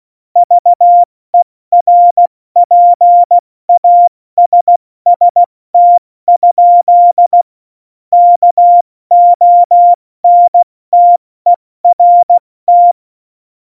Morse code, heard as a keyed tone, about 16 wpm, 700 Hz. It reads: VERPASST? KONTERT
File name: Morsekurs / Witze / Witz 11 Juli 17, Tempo 70 F80